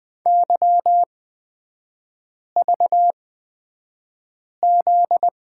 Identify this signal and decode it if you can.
Morse code, heard as a keyed tone, about 20 wpm, spaced as for 5 wpm, 700 Hz. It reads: YVZ
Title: 243 Lesson 39e - Letter z - Three Characters 20wpm